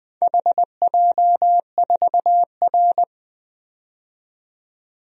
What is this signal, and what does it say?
Morse code, heard as a keyed tone, about 20 wpm, 700 Hz. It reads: HJ4R
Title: DXCC-100 Call Signs - 4 Characters - 20wpm